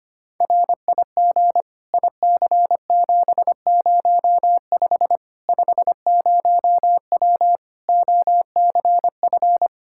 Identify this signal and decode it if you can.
Morse code, heard as a keyed tone, about 25 wpm, 700 Hz. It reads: RIG IC705 50W OCF
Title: Practice QSO - Exchange 2 - Info - 25wpm